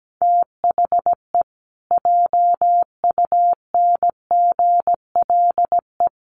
Morse code, heard as a keyed tone, about 17 wpm, 700 Hz. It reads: THE JUNGLE
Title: A Sound Of Thunder 17wpm